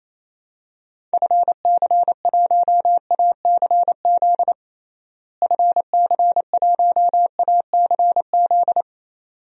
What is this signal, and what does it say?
Morse code, heard as a keyed tone, about 27 wpm, 700 Hz. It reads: FC1ACZ FC1ACZ
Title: DXCC-100 Call Signs - 6 Characters - Repeated 2x - 28wpm